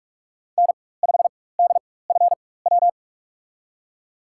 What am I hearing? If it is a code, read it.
Morse code, heard as a keyed tone, about 42 wpm, spaced as for 20 wpm, 700 Hz. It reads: N5DFW
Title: USA Call Signs - 5 Characters - 4x Character Spacing - 45wpm